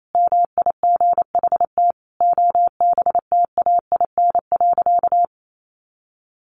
Morse code, heard as a keyed tone, about 28 wpm, 700 Hz. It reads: MIGHT OBTAIN.
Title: Sentences from Top 1300 Words 28wpm